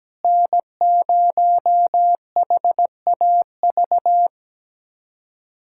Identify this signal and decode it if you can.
Morse code, heard as a keyed tone, about 17 wpm, 700 Hz. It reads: N0HAV
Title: USA Callsign DE Callsign - 5 Characters - 17wpm